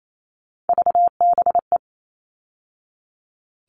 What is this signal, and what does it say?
Morse code, heard as a keyed tone, about 28 wpm, 700 Hz. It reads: VBE
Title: All Cut Numbers - 3-Digit 28wpm